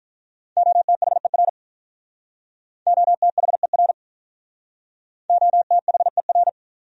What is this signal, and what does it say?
Morse code, heard as a keyed tone, about 45 wpm, 700 Hz. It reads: OTHER OTHER OTHER